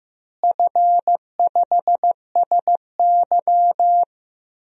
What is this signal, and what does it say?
Morse code, heard as a keyed tone, about 15 wpm, 700 Hz. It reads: F5SY
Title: DXCC-100 Call Signs - 4 Characters - Repeated 3x - 15wpm